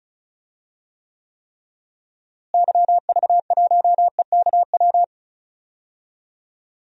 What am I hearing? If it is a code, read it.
Morse code, heard as a keyed tone, about 34 wpm, 700 Hz. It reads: YV1EKW
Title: DXCC-100 Call Signs - 6 Characters - Repeated 3x - 35wpm